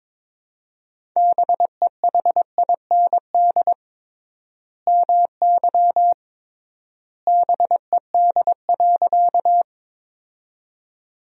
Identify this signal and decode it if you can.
Morse code, heard as a keyed tone, about 22 wpm, 700 Hz. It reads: BEHIND MY BED.
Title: 3x Word-Spacing - Sentences from Top 500 Words - 22wpm